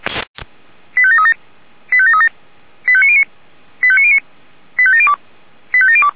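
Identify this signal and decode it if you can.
Five-tone selective call (ZVEI), heard as quick runs of five tones, tones of about 70 ms. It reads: 85737 85737 85909 85909 85702 85702
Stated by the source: ZVEI five-tone selective call received on an SDR.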